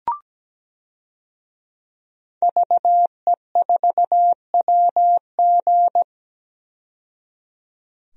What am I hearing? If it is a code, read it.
Morse code, heard as a keyed tone, about 17 wpm, 700 Hz. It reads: VE4WG
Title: DXCC-100 Call Signs - 5 Characters - 17wpm